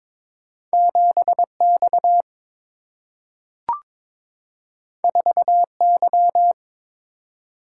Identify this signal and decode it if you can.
Morse code, heard as a keyed tone, about 22 wpm, 700 Hz. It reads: 7 X 4 Y
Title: Single-Letter Single-Number 22wpm